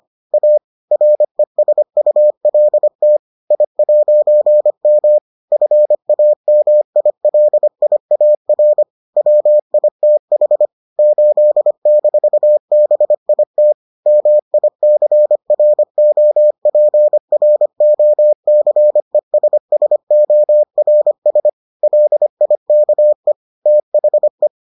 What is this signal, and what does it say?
Morse code, heard as a keyed tone, about 25 wpm, 600 Hz. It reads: A RESULT I'M FAMILIAR WITH 8-BIT MICROPROCESSORS LIKE THE